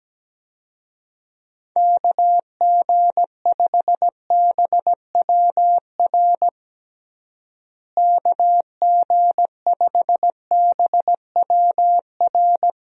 Morse code, heard as a keyed tone, about 17 wpm, 700 Hz. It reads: KG5BWR KG5BWR
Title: USA Call Signs - 6 Characters - Repeated 3x - 17wpm